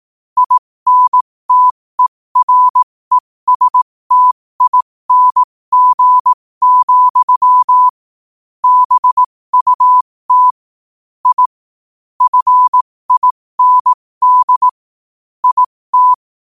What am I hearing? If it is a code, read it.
Morse code, heard as a keyed tone, about 18 wpm, spaced as for 15 wpm, 1000 Hz. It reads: INTERESTING, BUT I FIND IT
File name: qotdfiles / 15wpm